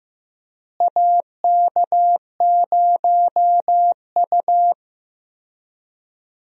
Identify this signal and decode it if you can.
Morse code, heard as a keyed tone, about 15 wpm, 700 Hz. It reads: AK0U
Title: USA Call Signs - 4 Characters - Repeated 3x - 15wpm